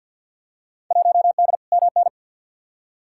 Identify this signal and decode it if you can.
Morse code, heard as a keyed tone, about 47 wpm, 700 Hz. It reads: 1D MN
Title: Field Day Exchanges - 50wpm